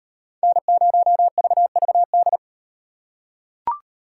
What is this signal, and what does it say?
Morse code, heard as a keyed tone, about 38 wpm, 700 Hz. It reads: N0VVD